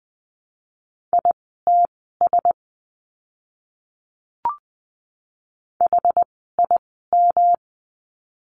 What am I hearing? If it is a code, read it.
Morse code, heard as a keyed tone, about 20 wpm, spaced as for 15 wpm, 700 Hz. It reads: ITS HIM
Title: 237 Lesson 38f - Letter q - Words 20wpm